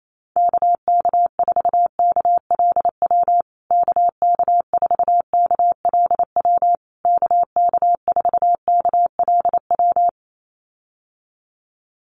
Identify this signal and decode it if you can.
Morse code, heard as a keyed tone, about 28 wpm, 700 Hz. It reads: KK4KLW KK4KLW KK4KLW
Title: USA Call Signs Repeated 3x 28wpm